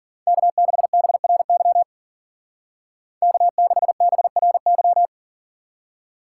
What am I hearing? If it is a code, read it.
Morse code, heard as a keyed tone, about 49 wpm, 700 Hz. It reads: K6BRY K6BRY
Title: Speed Racing - USA 1x3 Call Signs 40wpm